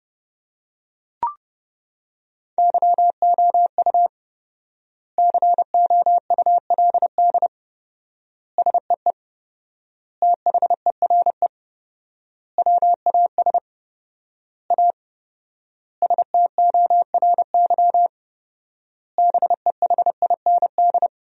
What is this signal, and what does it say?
Morse code, heard as a keyed tone, about 30 wpm, 700 Hz. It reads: YOU COULD SEE THERE WAS A STORY BEHIND
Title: 4x Word-Spacing - Sentences from Top 500 Words - 30wpm